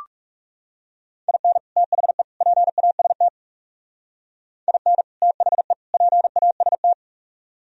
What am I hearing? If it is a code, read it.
Morse code, heard as a keyed tone, about 36 wpm, 700 Hz. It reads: IN THE PAST IN THE PAST
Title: Speed-Racing - Sets of 3 Words 30wpm